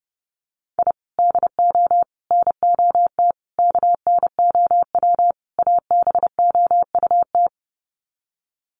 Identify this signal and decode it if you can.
Morse code, heard as a keyed tone, about 30 wpm, 700 Hz. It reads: I DO NOT KNOW ABOUT
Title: Sets of 5 Words 30wpm